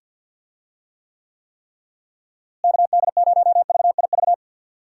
Morse code, heard as a keyed tone, about 45 wpm, 700 Hz. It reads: KD0VIV
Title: USA Call Signs - 6 Characters - Repeated 3x - 50wpm